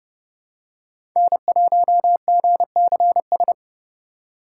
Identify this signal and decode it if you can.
Morse code, heard as a keyed tone, about 31 wpm, 700 Hz. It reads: N1GCS
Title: USA Call Signs - 5 Characters - 4x Character Spacing - 30wpm